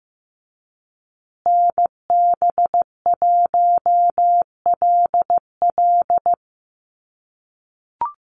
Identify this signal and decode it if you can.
Morse code, heard as a keyed tone, about 15 wpm, 700 Hz. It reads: NB1LL
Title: USA Call Signs Repeated 3x 15wpm